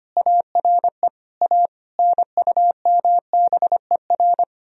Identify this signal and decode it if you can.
Morse code, heard as a keyed tone, about 30 wpm, 700 Hz. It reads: ARE A NUMBER
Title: Speed-Racing - Sets of 5 Words 25wpm